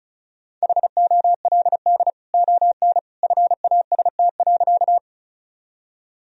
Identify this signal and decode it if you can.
Morse code, heard as a keyed tone, about 35 wpm, 700 Hz. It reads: HOLD ON FAST.
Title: Sentences from Top 1000 Words - Review Part 2 - 35wpm